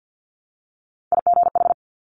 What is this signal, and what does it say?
Morse code, heard as a keyed tone, about 50 wpm, 700 Hz. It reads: IGH
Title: Top Three Letter Combinations in Words 50wpm